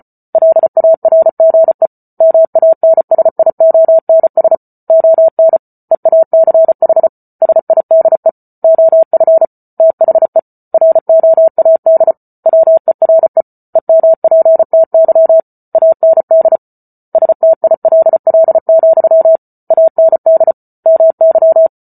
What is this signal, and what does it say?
Morse code, heard as a keyed tone, about 34 wpm, 650 Hz. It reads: LARGE MANSIONS ON EACH SIDE OF THE ROAD WERE EMPTY AND STILL, AND MY